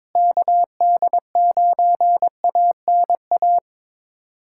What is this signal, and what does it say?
Morse code, heard as a keyed tone, about 22 wpm, 700 Hz. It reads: KD9ANA
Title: USA Callsign DE Callsign - 6 Characters - 22wpm